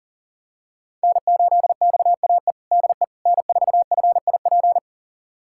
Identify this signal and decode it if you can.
Morse code, heard as a keyed tone, about 40 wpm, 700 Hz. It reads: N8XAE DE N4FIP
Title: USA Callsign DE Callsign - 5 Characters - 40wpm